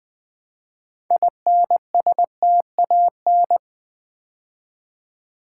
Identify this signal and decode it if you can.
Morse code, heard as a keyed tone, about 20 wpm, 700 Hz. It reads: INSTAN
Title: Build A Word - 3 Letters At A Time - 9-Letter Words - 20wpm